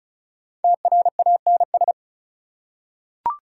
Morse code, heard as a keyed tone, about 36 wpm, 700 Hz. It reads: TRANS
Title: Prefixes and Suffixes 35wpm